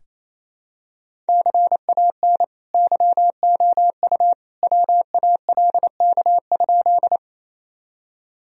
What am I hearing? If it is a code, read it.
Morse code, heard as a keyed tone, about 28 wpm, 700 Hz. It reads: CAN YOU WALK?
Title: Sentences from Top 1000 Words - Review Part 1 - 28wpm